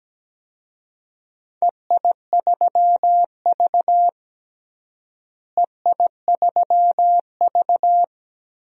Morse code, heard as a keyed tone, about 17 wpm, 700 Hz. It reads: EI3V EI3V
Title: DXCC-100 Call Signs - 4 Characters - Repeated 3x - 17wpm